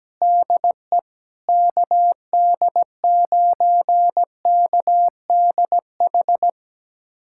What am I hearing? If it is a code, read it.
Morse code, heard as a keyed tone, about 17 wpm, 700 Hz. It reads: DE KD9KDH
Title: USA Callsign DE Callsign - 6 Characters - 17wpm